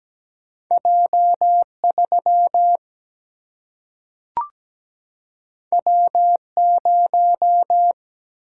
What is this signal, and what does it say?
Morse code, heard as a keyed tone, about 17 wpm, 700 Hz. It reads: J3 W0
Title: Single-Letter Single-Number 17wpm